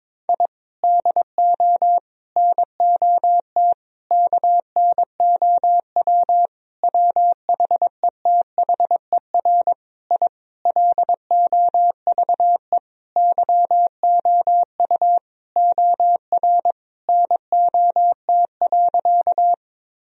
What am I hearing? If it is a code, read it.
Morse code, heard as a keyed tone, about 22 wpm, 700 Hz. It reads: I DO NOT KNOW WHETHER I LOVE YOU OR NOT.
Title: Sentences from Top 1000 Words - Review Part 3 - 22wpm